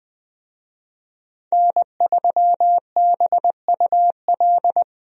Morse code, heard as a keyed tone, about 21 wpm, 700 Hz. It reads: N3BUL
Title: USA Call Signs - 5 Characters - 20wpm